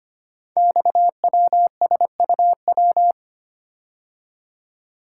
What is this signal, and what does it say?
Morse code, heard as a keyed tone, about 25 wpm, 700 Hz. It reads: XWSUW
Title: Code Groups - 5 Letters - 25wpm